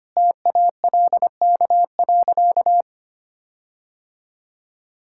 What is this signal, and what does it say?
Morse code, heard as a keyed tone, about 25 wpm, 700 Hz. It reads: TALK.
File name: practice / 25wpm / Sentences with 4-Word Phrases - 25wpm